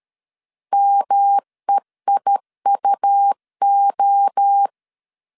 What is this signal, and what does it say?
Morse code, heard as a keyed tone, about 13 wpm, 800 Hz. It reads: MEIUO
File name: CW / DET / 1 3 1500C S C 12wpm 2750